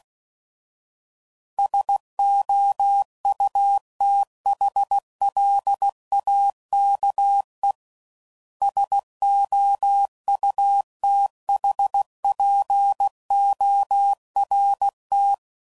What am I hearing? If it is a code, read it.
Morse code, heard as a keyed tone, about 16 wpm, 800 Hz. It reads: SOUTHLAKE SOUTHPORT
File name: CW Code Groups @15WPM